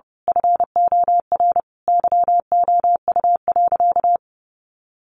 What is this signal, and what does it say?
Morse code, heard as a keyed tone, about 30 wpm, 700 Hz. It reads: FOR YOU.
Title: Sentences from Top 1300 Words 30wpm